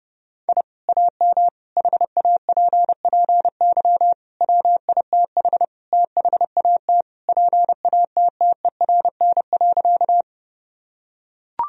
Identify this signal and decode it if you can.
Morse code, heard as a keyed tone, about 30 wpm, 700 Hz. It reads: I AM HAPPY WITH THAT PATTERN.
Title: Sentences from Top 1000 Words - Review Part 2 - 30wpm